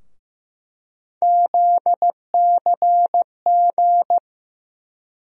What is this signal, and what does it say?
Morse code, heard as a keyed tone, about 15 wpm, 700 Hz. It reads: ZCG
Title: USA Call Sign Postfixes - 3 Letters 15wpm